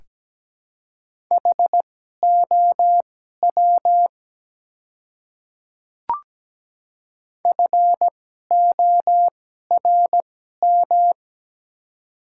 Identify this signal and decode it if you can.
Morse code, heard as a keyed tone, about 17 wpm, spaced as for 12 wpm, 700 Hz. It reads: HOW FORM
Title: 2x Character Spacing - Top 200 Words - 17wpm